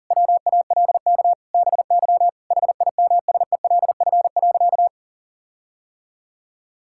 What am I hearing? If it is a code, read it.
Morse code, heard as a keyed tone, about 40 wpm, 700 Hz. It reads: WALK BY HIMSELF.
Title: Sentences from Top 1600 Words 40wpm